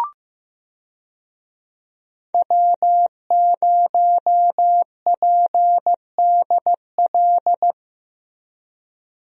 Morse code, heard as a keyed tone, about 15 wpm, 700 Hz. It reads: W0PDL
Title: USA Call Signs - 5 Characters - Repeated 3x - 15wpm